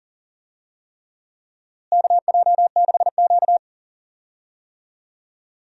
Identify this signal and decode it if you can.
Morse code, heard as a keyed tone, about 40 wpm, 700 Hz. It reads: KJ6Q
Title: USA Call Signs - 4 Characters - Repeated 3x - 40wpm